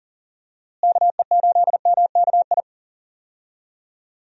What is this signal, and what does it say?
Morse code, heard as a keyed tone, about 40 wpm, 700 Hz. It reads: KE8MKI
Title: USA Call Signs - 6 Characters - Repeated 3x - 40wpm